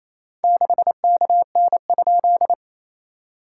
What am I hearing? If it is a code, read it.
Morse code, heard as a keyed tone, about 28 wpm, 700 Hz. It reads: 6KN?
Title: SOTA QSO Practice 28wpm